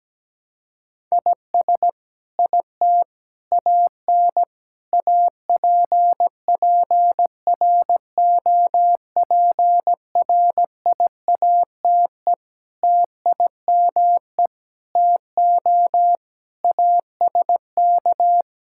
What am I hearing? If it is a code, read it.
Morse code, heard as a keyed tone, about 17 wpm, 700 Hz. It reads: IS IT AN APPROPRIATE TIME TO ASK
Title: Sentences from Top 1000 Words - Review Part 5 - 17wpm